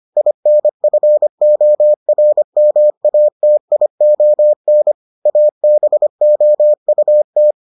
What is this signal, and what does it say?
Morse code, heard as a keyed tone, about 25 wpm, 600 Hz. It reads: INFORMATION ABOUT